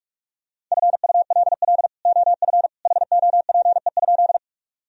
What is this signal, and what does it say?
Morse code, heard as a keyed tone, about 46 wpm, 700 Hz. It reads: FULL OF HOPE?
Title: Sentences from Top 1000 Words - Review Part 3 - 45wpm